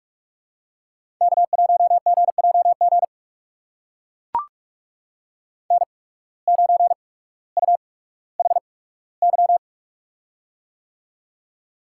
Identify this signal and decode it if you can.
Morse code, heard as a keyed tone, about 42 wpm, spaced as for 19 wpm, 700 Hz. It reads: K1GJG N9UHY
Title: USA Call Signs - 5 Characters - 8x Character Spacing - 45wpm